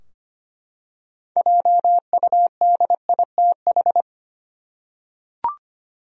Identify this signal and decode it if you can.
Morse code, heard as a keyed tone, about 25 wpm, 700 Hz. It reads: JUDITH
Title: USA Common Names - 25wpm